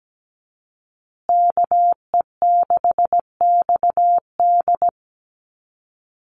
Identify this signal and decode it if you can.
Morse code, heard as a keyed tone, about 17 wpm, 700 Hz. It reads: KE6XD
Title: USA Call Signs Repeated 3x 17wpm